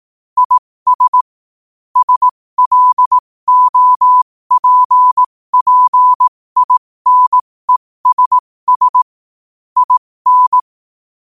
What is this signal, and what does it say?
Morse code, heard as a keyed tone, about 18 wpm, spaced as for 15 wpm, 1000 Hz. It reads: IS SLOPPINESS IN
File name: qotdfiles / 15wpm